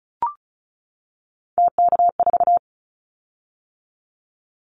Morse code, heard as a keyed tone, about 34 wpm, 700 Hz. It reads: TK4
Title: Callsign Prefixes for Top 100 DXCC - 35wpm